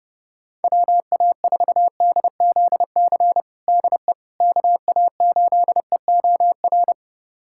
Morse code, heard as a keyed tone, about 30 wpm, 700 Hz. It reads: WA4DZC DE KA8EOR
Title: USA Callsign DE Callsign - 6 Characters - 30wpm